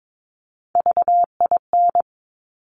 Morse code, heard as a keyed tone, about 22 wpm, 700 Hz. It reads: VIN
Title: Top Three Letter Combinations in Words 22wpm